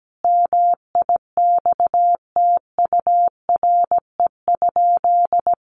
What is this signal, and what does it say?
Morse code, heard as a keyed tone, about 17 wpm, 700 Hz. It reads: MIXTURE?
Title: Sentences from Top 1500 Words 17wpm